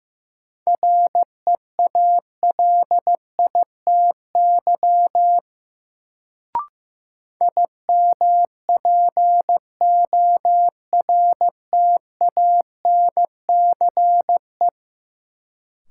Morse code, heard as a keyed tone, about 15 wpm, 700 Hz. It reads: REALITY IMPORTANCE
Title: Sentences from Top 1200 Words - New Word Drill - 15wpm